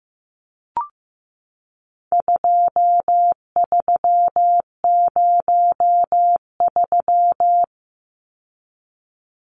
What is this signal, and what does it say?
Morse code, heard as a keyed tone, about 15 wpm, 700 Hz. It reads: 2303
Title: Numbers - 4-Digit 15wpm